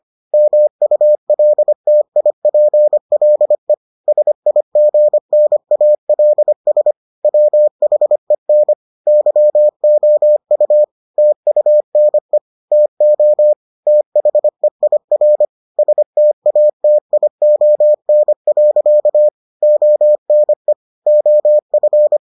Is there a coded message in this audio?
Yes: Morse code from 0.3 to 22.2 s, heard as a keyed tone, about 25 wpm, 600 Hz. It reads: MULTIPLE SIGNALS WHEN YOU TUNE TO THEIR STATION. ONE OF